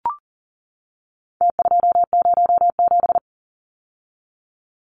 Morse code, heard as a keyed tone, about 40 wpm, 700 Hz. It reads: T207
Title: Numbers - Contest Style - 4-Digit 40wpm